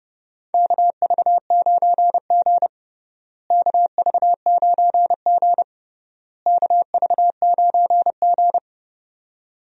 Morse code, heard as a keyed tone, about 30 wpm, 700 Hz. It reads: KV9G KV9G KV9G
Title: USA Call Signs - 4 Characters - Repeated 3x - 30wpm